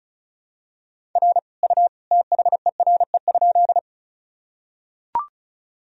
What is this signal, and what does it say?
Morse code, heard as a keyed tone, about 36 wpm, 700 Hz. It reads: R U THERE?
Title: Sentences from Top 100 Words and Abbreviations 35wpm